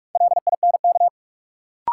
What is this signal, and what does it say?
Morse code, heard as a keyed tone, about 46 wpm, 700 Hz. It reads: LINK
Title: Sentences from Top 1500 Words - New Word Drill Alt - 45wpm